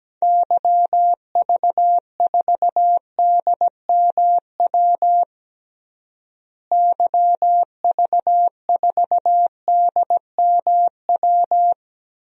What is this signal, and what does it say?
Morse code, heard as a keyed tone, about 17 wpm, 700 Hz. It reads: YV4DMW YV4DMW
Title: DXCC-100 Call Signs - 6 Characters - Repeated 2x - 17wpm